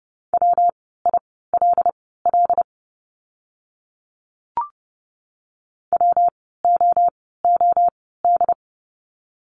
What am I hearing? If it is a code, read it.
Morse code, heard as a keyed tone, about 30 wpm, spaced as for 17 wpm, 700 Hz. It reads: WILL WOOD
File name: course / 30:10wpm / 115 Lesson 19f - Letter w - Words 30wpm